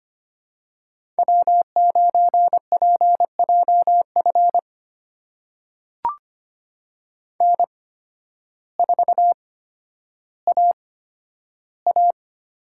Morse code, heard as a keyed tone, about 25 wpm, spaced as for 12 wpm, 700 Hz. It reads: W9PJF N4AA
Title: USA Call Signs - 5 Characters - 8x Character Spacing - 25wpm